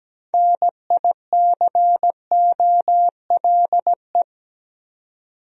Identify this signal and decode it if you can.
Morse code, heard as a keyed tone, about 17 wpm, 700 Hz. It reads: NICOLE
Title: USA Common Names - 17wpm